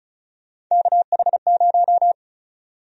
Morse code, heard as a keyed tone, about 36 wpm, 700 Hz. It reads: KH0
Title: USA Call Sign Prefixes - Repeated 2x - 35wpm